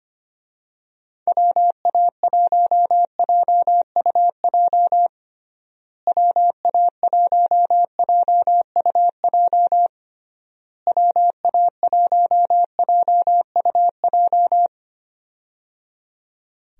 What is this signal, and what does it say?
Morse code, heard as a keyed tone, about 25 wpm, 700 Hz. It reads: WA1JUJ WA1JUJ WA1JUJ
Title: USA Call Signs - 6 Characters - Repeated 3x - 25wpm